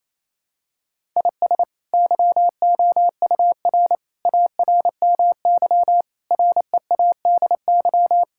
Morse code, heard as a keyed tone, about 29 wpm, 700 Hz. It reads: IS YOUR ARMY READY
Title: Sentences from Top 1000 Words - Review Part 5 - 28wpm